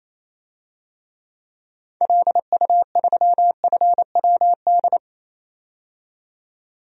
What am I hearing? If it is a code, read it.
Morse code, heard as a keyed tone, about 27 wpm, 700 Hz. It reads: LU3FWD
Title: DXCC-100 Call Signs - 6 Characters - 28wpm